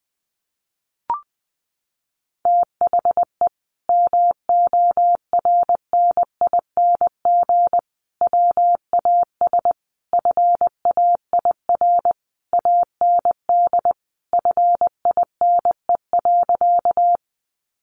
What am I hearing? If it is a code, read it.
Morse code, heard as a keyed tone, about 20 wpm, 700 Hz. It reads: THE MORNING WAS FAIR AND FINE.
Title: Sentences from Top 1300 Words 20wpm